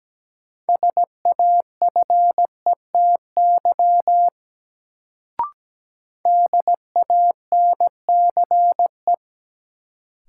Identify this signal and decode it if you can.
Morse code, heard as a keyed tone, about 17 wpm, 700 Hz. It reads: SAFETY DANCE
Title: Sentences from Top 700 Words - New Word Drill - 17wpm